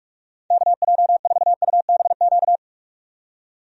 Morse code, heard as a keyed tone, about 45 wpm, 700 Hz. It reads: KJ4UBQ
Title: USA Call Signs - 6 Characters - 45wpm